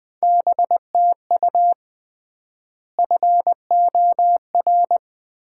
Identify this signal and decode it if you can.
Morse code, heard as a keyed tone, about 21 wpm, 700 Hz. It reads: BTU FOR
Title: Common QSO Phrases - 3x Word-Spacing - 20wpm